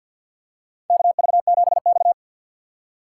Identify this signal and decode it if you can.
Morse code, heard as a keyed tone, about 53 wpm, 700 Hz. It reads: KV7X
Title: USA Call Signs - 4 Characters - 3x Character Spacing - 50wpm